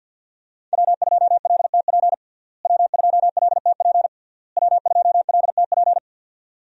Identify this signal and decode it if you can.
Morse code, heard as a keyed tone, about 52 wpm, 700 Hz. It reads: W2LTP W2LTP W2LTP
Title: USA Call Signs - 5 Characters - Repeated 3x - 50wpm